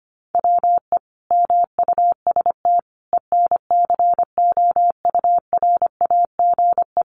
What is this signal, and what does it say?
Morse code, heard as a keyed tone, about 25 wpm, 700 Hz. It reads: WE MUST ENCOURAGE
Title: Sentences from Top 1400 Words 25wpm